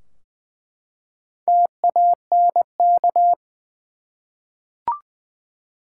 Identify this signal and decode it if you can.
Morse code, heard as a keyed tone, about 20 wpm, 700 Hz. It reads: TANK
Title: Four Letter Words - 20wpm